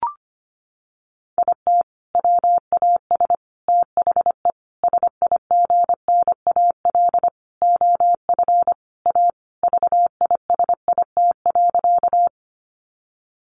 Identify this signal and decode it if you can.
Morse code, heard as a keyed tone, about 25 wpm, 700 Hz. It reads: IT WAS THE SIGNAL OF A VISIT.
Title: Sentences from Top 1400 Words 25wpm